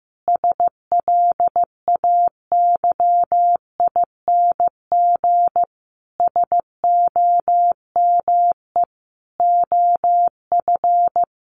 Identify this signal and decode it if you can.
Morse code, heard as a keyed tone, about 15 wpm, 700 Hz. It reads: SLAYING SOME OF
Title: A Sound Of Thunder 15wpm